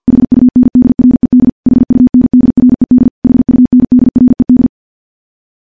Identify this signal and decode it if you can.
Morse code, heard as a keyed tone, about 50 wpm, 250 Hz. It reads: HUNDRED HUNDRED HUNDRED
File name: morse / morse-training-50WPM